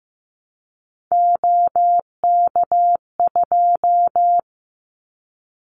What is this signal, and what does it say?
Morse code, heard as a keyed tone, about 15 wpm, 700 Hz. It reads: OK2
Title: Callsign Prefixes for Top 25 DXCC - 15wpm